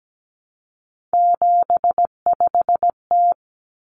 Morse code, heard as a keyed tone, about 17 wpm, 700 Hz. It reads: 75T
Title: Common Cut Numbers - 3-Digit 17wpm